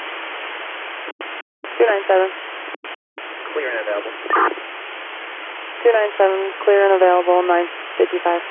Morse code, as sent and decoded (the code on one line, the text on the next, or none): none